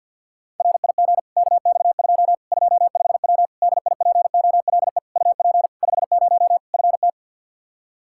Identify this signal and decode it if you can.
Morse code, heard as a keyed tone, about 50 wpm, 700 Hz. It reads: RIG KX2 25W DIPOLE UP 50 FT
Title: K1USN SST QSO Practice 50wpm